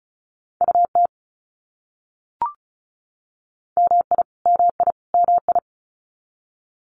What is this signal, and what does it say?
Morse code, heard as a keyed tone, about 34 wpm, 700 Hz. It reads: UT MI MI MI
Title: US States Repeated 3x 35wpm